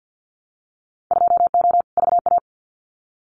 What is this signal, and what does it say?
Morse code, heard as a keyed tone, about 50 wpm, 700 Hz. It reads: 2O VA
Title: Winter Field Day Exchanges 50wpm